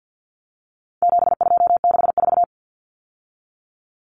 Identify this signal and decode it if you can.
Morse code, heard as a keyed tone, about 48 wpm, 700 Hz. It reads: 7264
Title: Numbers - 4-Digit 50wpm